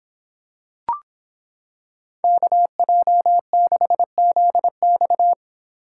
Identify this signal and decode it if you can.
Morse code, heard as a keyed tone, about 25 wpm, 700 Hz. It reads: KJ6ZX